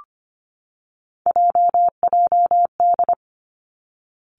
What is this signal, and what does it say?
Morse code, heard as a keyed tone, about 25 wpm, 700 Hz. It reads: JJD
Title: Three Challenging Letter-Number Combinations 25wpm